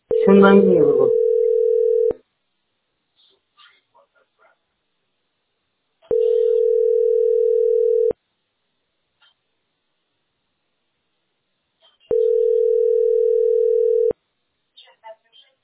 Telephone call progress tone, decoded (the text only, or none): ringback tone